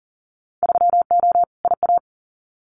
Morse code, heard as a keyed tone, about 41 wpm, 700 Hz. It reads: 3O IA
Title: Winter Field Day Exchanges 40wpm